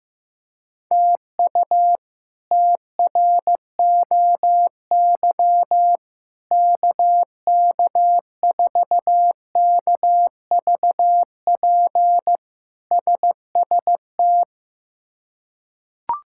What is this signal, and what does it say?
Morse code, heard as a keyed tone, about 15 wpm, 700 Hz. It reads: TU TROY KK4KVP SST
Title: K1USN SST QSO Practice - 15wpm